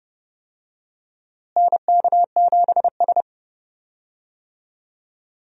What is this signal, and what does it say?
Morse code, heard as a keyed tone, about 30 wpm, 700 Hz. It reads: NK7S
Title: USA Call Signs - 4 Characters - Repeated 3x - 30wpm